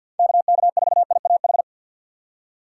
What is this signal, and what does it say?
Morse code, heard as a keyed tone, about 53 wpm, 700 Hz. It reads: KK4IAH
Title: USA Call Signs - 6 Characters - 3x Character Spacing - 50wpm